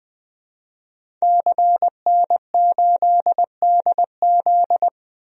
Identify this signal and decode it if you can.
Morse code, heard as a keyed tone, about 21 wpm, 700 Hz. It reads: CN8DZ
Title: DXCC-100 Call Signs - 5 Characters - 20wpm